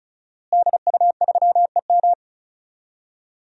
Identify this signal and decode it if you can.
Morse code, heard as a keyed tone, about 36 wpm, 700 Hz. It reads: DU3EM
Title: DXCC-100 Call Signs - 5 Characters - Repeated 3x - 35wpm